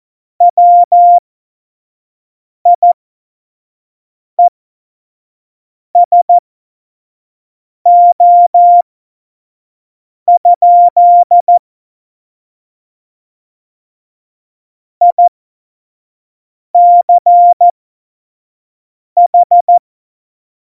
Morse code, heard as a keyed tone, about 14 wpm, spaced as for 5 wpm, 700 Hz. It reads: WIESO? ICH
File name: Morsekurs / Witze / Witz 25. Juni 2019, 25 F70